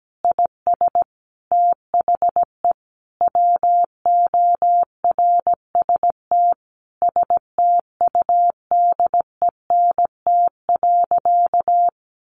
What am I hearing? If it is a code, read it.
Morse code, heard as a keyed tone, about 17 wpm, 700 Hz. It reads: IS THE WORST STUDENT.
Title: Additional Easy Phrases 17wpm